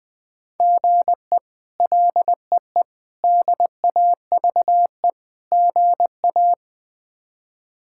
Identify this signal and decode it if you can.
Morse code, heard as a keyed tone, about 20 wpm, 700 Hz. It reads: GE LEE DAVE GA
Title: K1USN SST QSO Practice - 20wpm